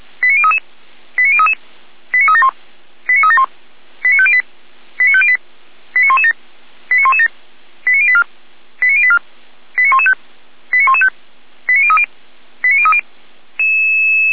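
Five-tone selective call (ZVEI), heard as quick runs of five tones, tones of about 70 ms. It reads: 89033 89033 89371 89371 89508 89508 89117 89117 89084 89084 89115 89115 89030 89030
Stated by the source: ZVEI five-tone selective call received by SDR, fire brigade Orlamünde (feuerwehr-orlamuende) server.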